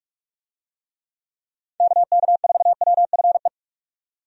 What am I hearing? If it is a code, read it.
Morse code, heard as a keyed tone, about 42 wpm, 700 Hz. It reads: KK4WFE
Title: USA Call Signs - 6 Characters - 45wpm